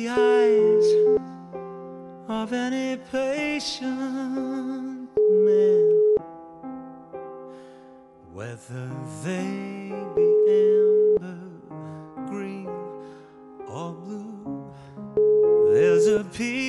Ringback tone (Europe, 425 Hz).